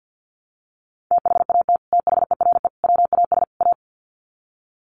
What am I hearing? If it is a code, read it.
Morse code, heard as a keyed tone, about 50 wpm, 700 Hz. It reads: THAT THERE WAS A
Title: Sets of 4 Words 50wpm